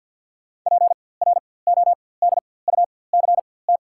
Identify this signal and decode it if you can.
Morse code, heard as a keyed tone, about 53 wpm, spaced as for 24 wpm, 700 Hz. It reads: PRODUCT
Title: 4x Character Spacing - Top 500 Words - 50wpm